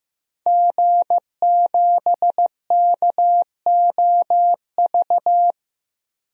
Morse code, heard as a keyed tone, about 15 wpm, 700 Hz. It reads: G7KOV
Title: DXCC-100 Call Signs - 5 Characters - 15wpm